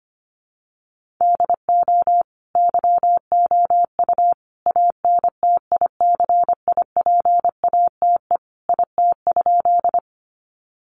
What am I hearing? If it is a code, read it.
Morse code, heard as a keyed tone, about 25 wpm, 700 Hz. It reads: DO YOU ANTICIPATE IT?
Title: Sentences from Top 1500 Words 25wpm